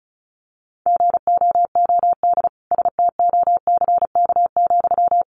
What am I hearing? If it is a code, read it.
Morse code, heard as a keyed tone, about 34 wpm, 700 Hz. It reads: GOOD STOCK,
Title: Sentences from Top 800 Words 35wpm